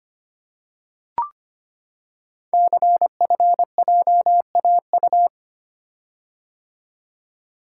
Morse code, heard as a keyed tone, about 25 wpm, 700 Hz. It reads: CFJAU
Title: Code Groups - 5 Letters - 25wpm